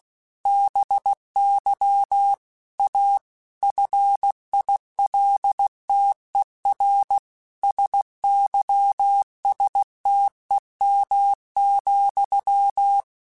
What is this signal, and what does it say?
Morse code, heard as a keyed tone, about 16 wpm, 800 Hz. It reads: BY A FILTER SYSTEM,
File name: Technical Subjects / Technical Subjects-mp3s / Sw Pwr Sup @15